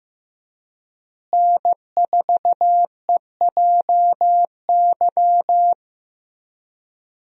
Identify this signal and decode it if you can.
Morse code, heard as a keyed tone, about 15 wpm, 700 Hz. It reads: N4EJY
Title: USA Call Signs - 5 Characters - Repeated 3x - 15wpm